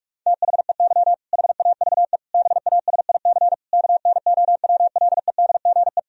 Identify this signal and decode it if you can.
Morse code, heard as a keyed tone, about 41 wpm, 700 Hz. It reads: THEY HAVE BASIC KNOWLEDGE
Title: Sentences from Top 1000 Words - Review Part 3 - 45wpm